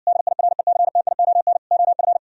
Morse code, heard as a keyed tone, about 59 wpm, 700 Hz. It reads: DIRECTION OF